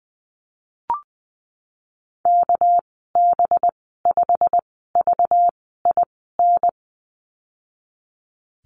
Morse code, heard as a keyed tone, about 20 wpm, spaced as for 15 wpm, 700 Hz. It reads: KB5VIN
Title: 238 Lesson 38g - Letter q - Call Signs 20wpm